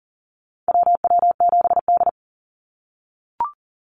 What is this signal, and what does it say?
Morse code, heard as a keyed tone, about 37 wpm, 700 Hz. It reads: WW7D
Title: NA SOTA Activator Callsigns 40wpm